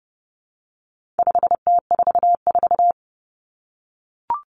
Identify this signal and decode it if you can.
Morse code, heard as a keyed tone, about 30 wpm, 700 Hz. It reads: 5T44
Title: Common Cut Numbers - 4-Digit 30wpm